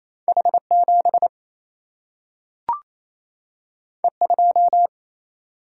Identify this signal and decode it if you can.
Morse code, heard as a keyed tone, about 28 wpm, 700 Hz. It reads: H 7 E 2
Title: Single-Letter Single-Number 28wpm